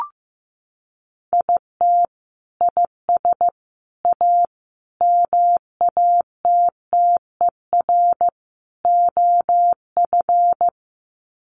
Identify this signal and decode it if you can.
Morse code, heard as a keyed tone, about 15 wpm, 700 Hz. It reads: IT IS A MATTER OF
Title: Sets of 5 Words Encore2 15wpm